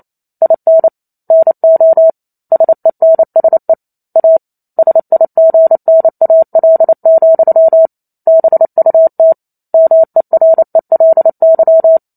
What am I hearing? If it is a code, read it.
Morse code, heard as a keyed tone, about 28 wpm, 650 Hz. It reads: IN NO SENSE A SIGNAL, BUT MERELY
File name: Chapter 2 - 28 WPM